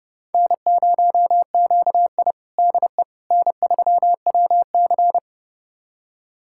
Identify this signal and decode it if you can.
Morse code, heard as a keyed tone, about 30 wpm, 700 Hz. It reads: N0QI DE N3WC
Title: USA Callsign DE Callsign - 4 Characters - 30wpm